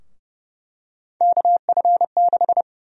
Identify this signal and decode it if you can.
Morse code, heard as a keyed tone, about 31 wpm, 700 Hz. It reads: KF6
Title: USA Call Sign Prefixes - 30wpm